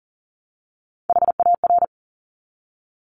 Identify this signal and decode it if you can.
Morse code, heard as a keyed tone, about 41 wpm, 690 Hz. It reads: HAR
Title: Top Three Letter Combinations in Words 40wpm